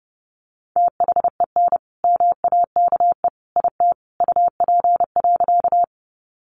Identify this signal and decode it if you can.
Morse code, heard as a keyed tone, about 30 wpm, 700 Hz. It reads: THEN MAKE IT UP.
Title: Sentences from Top 100 Words 30wpm